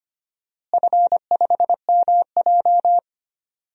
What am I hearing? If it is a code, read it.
Morse code, heard as a keyed tone, about 25 wpm, 700 Hz. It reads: F5MJ
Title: DXCC-100 Call Signs - 4 Characters - Repeated 2x - 25wpm